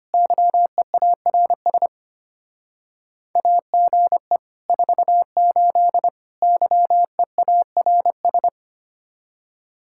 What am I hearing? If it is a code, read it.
Morse code, heard as a keyed tone, about 33 wpm, 700 Hz. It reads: YEARS AGE 48 YEARS